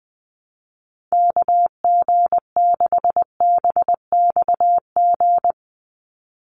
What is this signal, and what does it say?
Morse code, heard as a keyed tone, about 20 wpm, 700 Hz. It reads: KG6BXG
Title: USA Call Signs 20wpm